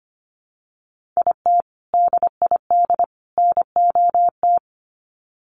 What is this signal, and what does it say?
Morse code, heard as a keyed tone, about 25 wpm, 700 Hz. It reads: IT DID NOT
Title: Sets of 3 Words 25wpm